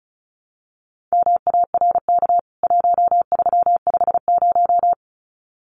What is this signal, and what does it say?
Morse code, heard as a keyed tone, about 34 wpm, 700 Hz. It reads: MARK 1350
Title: CWT Exchange 35wpm